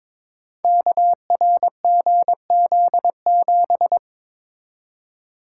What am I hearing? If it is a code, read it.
Morse code, heard as a keyed tone, about 22 wpm, 700 Hz. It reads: KRGZ7
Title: Code Groups - 5 Letter:Numbers - 22wpm